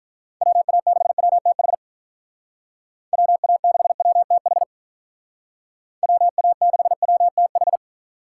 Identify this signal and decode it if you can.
Morse code, heard as a keyed tone, about 46 wpm, 700 Hz. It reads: WA6WTH WA6WTH WA6WTH